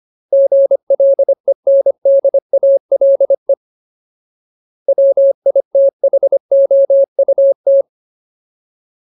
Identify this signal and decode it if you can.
Morse code, heard as a keyed tone, about 25 wpm, 550 Hz. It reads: GLENDALE WITHOUT
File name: Ultimate Word List / 25WPM Ultimate